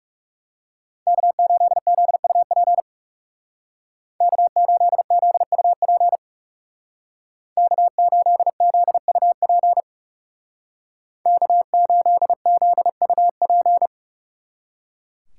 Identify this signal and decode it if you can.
Morse code, heard as a keyed tone, about 35 wpm, 700 Hz. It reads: K8ZUP K8ZUP K8ZUP K8ZUP
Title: Speed Racing - USA Call Signs 30wpm